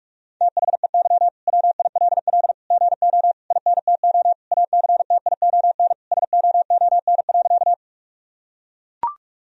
Morse code, heard as a keyed tone, about 42 wpm, 700 Hz. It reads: THEY WILL GO INTO ACTION SOON.
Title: Sentences from Top 1000 Words - Review Part 4 - 45wpm